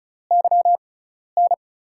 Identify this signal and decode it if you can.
Morse code, heard as a keyed tone, about 36 wpm, spaced as for 12 wpm, 700 Hz. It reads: YN
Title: USA Call Signs - 6 Characters - 6x Character Spacing - 35wpm